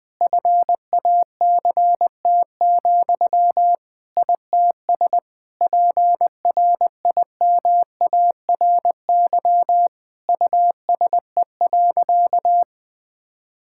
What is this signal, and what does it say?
Morse code, heard as a keyed tone, about 20 wpm, 700 Hz. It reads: FACT, ITS PRIMARY USE.
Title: Sentences from Top 1000 Words - Review Part 5 - 20wpm